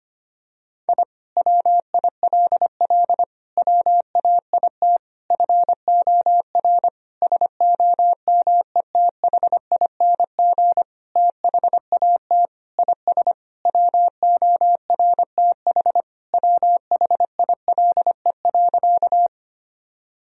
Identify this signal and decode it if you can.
Morse code, heard as a keyed tone, about 25 wpm, 700 Hz. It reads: I WILL WAIT FOR SOMETHING THAT IS WORTH WHILE.
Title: Sentences from Top 1000 Words - Review Part 1 - 25wpm